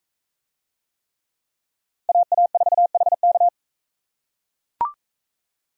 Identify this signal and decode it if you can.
Morse code, heard as a keyed tone, about 38 wpm, 700 Hz. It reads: AA4HK